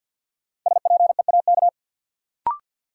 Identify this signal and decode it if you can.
Morse code, heard as a keyed tone, about 53 wpm, 700 Hz. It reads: SPEAK
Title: Sentences from Top 1100 Words - New Word Drill Alt - 50wpm